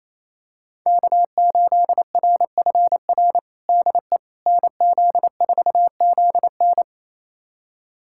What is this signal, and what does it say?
Morse code, heard as a keyed tone, about 27 wpm, 700 Hz. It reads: K8RFR DE NZ4ZN
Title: USA Callsign DE Callsign - 5 Characters - 28wpm